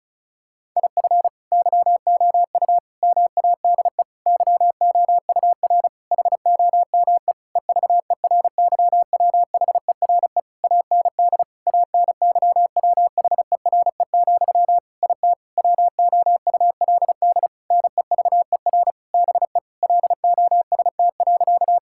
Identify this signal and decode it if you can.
Morse code, heard as a keyed tone, about 36 wpm, 700 Hz. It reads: IF YOU MADE YOUR HOME EVERYWHERE AND ANYWHERE, IT WOULD NEVER BE LOST.
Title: Sentences from Top 1600 Words 35wpm